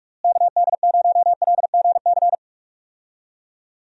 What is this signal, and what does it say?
Morse code, heard as a keyed tone, about 47 wpm, 700 Hz. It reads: KD0LGC
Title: USA Call Signs - 6 Characters - Repeated 2x - 45wpm